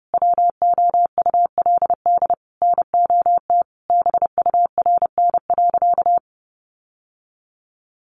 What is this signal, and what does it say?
Morse code, heard as a keyed tone, about 30 wpm, 700 Hz. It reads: WOULD NOT BURN.
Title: Sentences from Top 1300 Words 30wpm